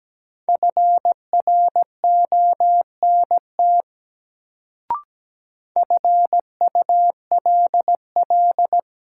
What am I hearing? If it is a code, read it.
Morse code, heard as a keyed tone, about 17 wpm, 700 Hz. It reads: FRONT FULL
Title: Sentences from Top 500 Words - New Word Drill - 17wpm